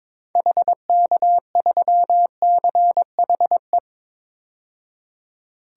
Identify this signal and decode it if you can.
Morse code, heard as a keyed tone, about 22 wpm, 700 Hz. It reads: HK3CHE
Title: DXCC-100 Call Signs - 6 Characters - 22wpm